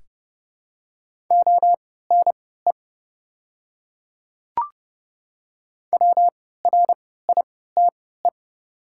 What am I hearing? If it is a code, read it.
Morse code, heard as a keyed tone, about 30 wpm, spaced as for 17 wpm, 700 Hz. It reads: ONE WRITE
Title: 3x Character Spacing - Top 500 Words - 30wpm